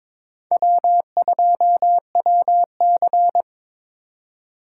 Morse code, heard as a keyed tone, about 23 wpm, 700 Hz. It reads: W2WC
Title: USA Call Signs - 4 Characters - 8x Character Spacing - 22wpm